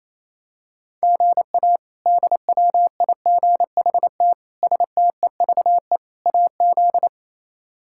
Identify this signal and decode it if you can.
Morse code, heard as a keyed tone, about 28 wpm, 700 Hz. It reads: GA DWIGHT STEVE AZ
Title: K1USN SST QSO Practice - 28wpm